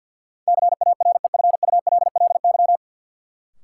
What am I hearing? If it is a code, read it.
Morse code, heard as a keyed tone, about 50 wpm, 700 Hz. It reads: CAREFULLY
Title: Sentences from Top 1300 Words - New Word Drill Alt - 50wpm